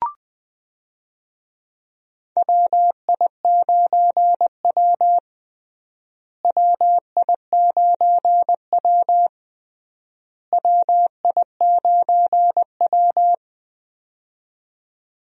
Morse code, heard as a keyed tone, about 20 wpm, 700 Hz. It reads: WI9W WI9W WI9W
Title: USA Call Signs - 4 Characters - Repeated 3x - 20wpm